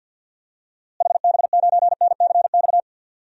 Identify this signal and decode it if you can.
Morse code, heard as a keyed tone, about 51 wpm, 700 Hz. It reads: HB9NCX
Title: DXCC-100 Call Signs - 6 Characters - Repeated 2x - 50wpm